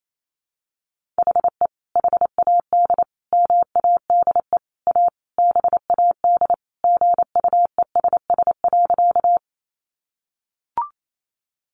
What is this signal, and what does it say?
Morse code, heard as a keyed tone, about 28 wpm, 700 Hz. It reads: HE HAD MADE A BAD GUESS.
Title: Sentences from Top 1500 Words 28wpm